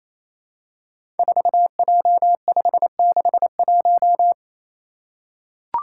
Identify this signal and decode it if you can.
Morse code, heard as a keyed tone, about 29 wpm, 700 Hz. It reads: 4J561
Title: Five Challenging Letter-Number Combinations 28wpm